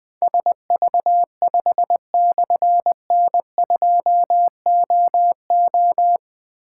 Morse code, heard as a keyed tone, about 21 wpm, 700 Hz. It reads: SV5/N2OO
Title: Difficult Call Signs 20wpm